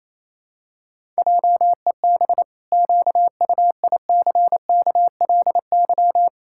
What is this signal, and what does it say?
Morse code, heard as a keyed tone, about 28 wpm, 700 Hz. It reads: JEB QUICKLY
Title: Pangram Challenge v2 - 28wpm